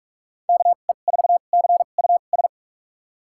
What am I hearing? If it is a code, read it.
Morse code, heard as a keyed tone, about 45 wpm, spaced as for 32 wpm, 700 Hz. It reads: KE4CUS
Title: USA Call Signs - 6 Characters - 2x Character Spacing - 45wpm